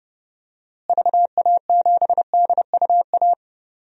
Morse code, heard as a keyed tone, about 30 wpm, 700 Hz. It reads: VA7DUA
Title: DXCC-100 Call Signs - 6 Characters - Repeated 2x - 30wpm